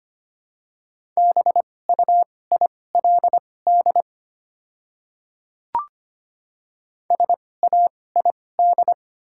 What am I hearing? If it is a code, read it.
Morse code, heard as a keyed tone, about 25 wpm, spaced as for 18 wpm, 700 Hz. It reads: BUILD SAID
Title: 2x Character Spacing - Top 200 Words - 25wpm